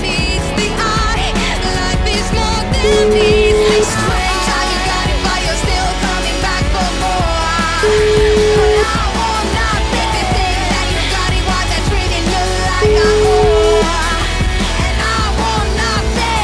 Ringback tone (Europe, 425 Hz), with noise.